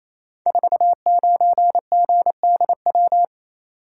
Morse code, heard as a keyed tone, about 29 wpm, 700 Hz. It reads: 49GDW
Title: Code Groups - 5 Letter:Numbers - 28wpm